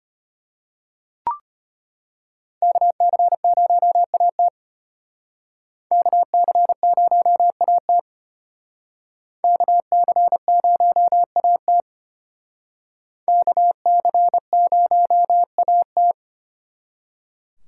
Morse code, heard as a keyed tone, about 32 wpm, 700 Hz. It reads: KC0AT KC0AT KC0AT KC0AT